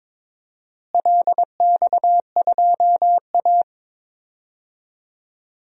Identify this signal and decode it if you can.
Morse code, heard as a keyed tone, about 22 wpm, 700 Hz. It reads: LX2A
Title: DXCC-100 Call Signs - 4 Characters - Repeated 2x - 22wpm